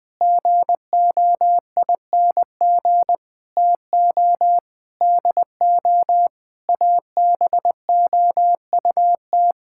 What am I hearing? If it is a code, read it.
Morse code, heard as a keyed tone, about 20 wpm, 700 Hz. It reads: GOING TO DO ABOUT
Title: Sentences with 6-Word Phrases - 20wpm